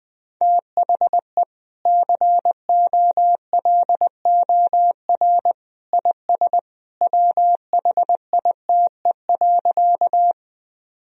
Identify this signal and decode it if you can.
Morse code, heard as a keyed tone, about 21 wpm, 700 Hz. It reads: THE COLOR IS WHITE.
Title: Sentences from Top 1000 Words - Review Part 2 - 20wpm